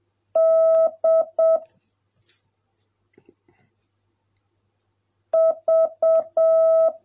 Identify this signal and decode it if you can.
Morse code, heard as a keyed tone, about 7 wpm, 650 Hz. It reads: D V